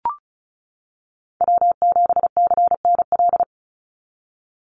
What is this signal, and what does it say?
Morse code, heard as a keyed tone, about 34 wpm, 700 Hz. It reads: W7CNL
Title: NA SOTA Chaser Callsigns 35wpm